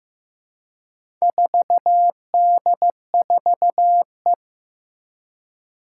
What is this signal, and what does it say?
Morse code, heard as a keyed tone, about 15 wpm, 700 Hz. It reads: 4D4E
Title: Four Challenging Letter-Number Combinations 15wpm